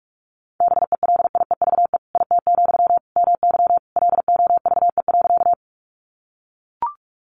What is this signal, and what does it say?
Morse code, heard as a keyed tone, about 44 wpm, 700 Hz. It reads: BELIEVE IT, MY LOVE.
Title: Sentences from Top 1000 Words 45wpm